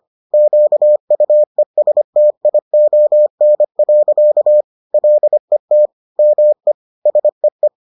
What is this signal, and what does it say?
Morse code, heard as a keyed tone, about 25 wpm, 600 Hz. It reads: QUESTION. LET ME SEE